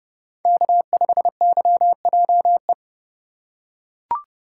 Morse code, heard as a keyed tone, about 28 wpm, 700 Hz. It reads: K5YJE